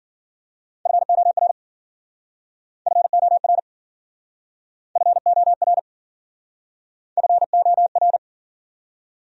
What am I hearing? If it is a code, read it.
Morse code, heard as a keyed tone, about 52 wpm, 700 Hz. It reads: FOR FOR FOR FOR